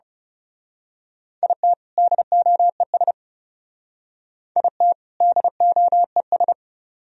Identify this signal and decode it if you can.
Morse code, heard as a keyed tone, about 34 wpm, 700 Hz. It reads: IT DOES IT DOES
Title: Speed-Racing - Sets of 2 Words 30wpm